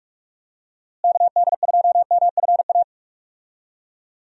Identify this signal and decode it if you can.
Morse code, heard as a keyed tone, about 46 wpm, 700 Hz. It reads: KD2MFA
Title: USA Call Signs - 6 Characters - Repeated 3x - 45wpm